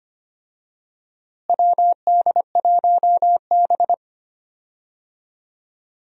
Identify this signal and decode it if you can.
Morse code, heard as a keyed tone, about 25 wpm, 700 Hz. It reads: WD1B
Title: USA Call Signs - 4 Characters - Repeated 2x - 25wpm